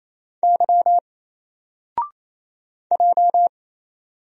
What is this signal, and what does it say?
Morse code, heard as a keyed tone, about 27 wpm, 700 Hz. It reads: Y J
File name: Single-Letter-Number-28wpm